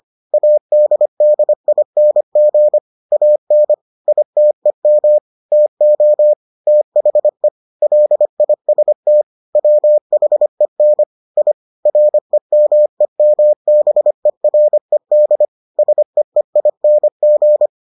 Morse code, heard as a keyed tone, about 25 wpm, 600 Hz. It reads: ADDING AN ITEM TO THE LIST WHEN I REMEMBERED SEEING